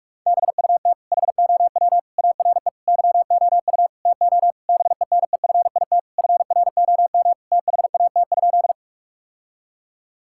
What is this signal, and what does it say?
Morse code, heard as a keyed tone, about 46 wpm, 700 Hz. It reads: BUT HOW ARE YOU TO BENEFIT FROM THAT?
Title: Sentences from Top 1000 Words - Review Part 6 - 45wpm